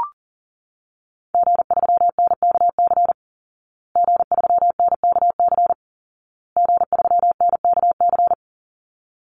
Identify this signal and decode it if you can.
Morse code, heard as a keyed tone, about 40 wpm, 700 Hz. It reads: G3NKC G3NKC G3NKC
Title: Call Signs Repeated 2x - 40wpm